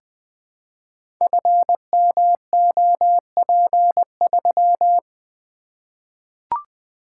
Morse code, heard as a keyed tone, about 20 wpm, 700 Hz. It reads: FMOP3
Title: Code Groups - 5 Letter:Numbers - 20wpm